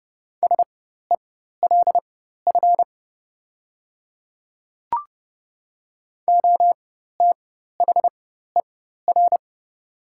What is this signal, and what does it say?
Morse code, heard as a keyed tone, about 30 wpm, spaced as for 14 wpm, 700 Hz. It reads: SELF OTHER
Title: 4x Character Spacing - Top 200 Words - 30wpm